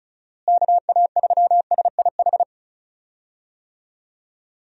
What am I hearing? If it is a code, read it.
Morse code, heard as a keyed tone, about 36 wpm, 700 Hz. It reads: KA3SIH
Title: USA Call Signs - 6 Characters - Repeated 3x - 35wpm